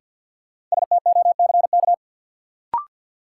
Morse code, heard as a keyed tone, about 53 wpm, 700 Hz. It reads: STOCK
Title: Sets of 5 Words - New Word Drill - 50wpm